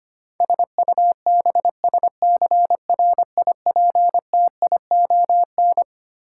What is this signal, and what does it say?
Morse code, heard as a keyed tone, about 25 wpm, 700 Hz. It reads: SUBSCRIPTION
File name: Twelve Letter Words - 25wpm